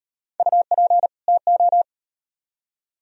Morse code, heard as a keyed tone, about 36 wpm, 700 Hz. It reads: UP TO